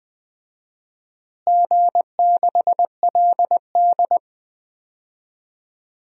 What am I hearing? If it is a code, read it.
Morse code, heard as a keyed tone, about 20 wpm, 700 Hz. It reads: G6LD
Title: DXCC-100 Call Signs - 4 Characters - Repeated 2x - 20wpm